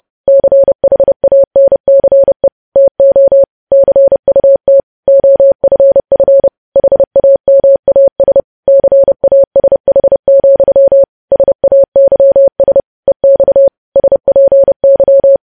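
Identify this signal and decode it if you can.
Morse code, heard as a keyed tone, about 30 wpm, 550 Hz. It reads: CHANCE TO CUT OFF HAMAS CASH, SAYS EX SPY
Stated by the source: News headlines sent in Morse at 30.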